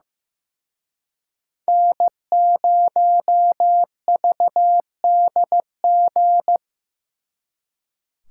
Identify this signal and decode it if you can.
Morse code, heard as a keyed tone, about 18 wpm, 700 Hz. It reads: N0VDG
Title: Speed Racing - USA 1x3 Call Signs 15wpm